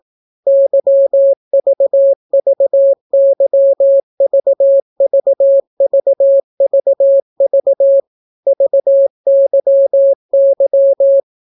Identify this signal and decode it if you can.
Morse code, heard as a keyed tone, about 18 wpm, 550 Hz. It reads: YVVYVVVVV VYY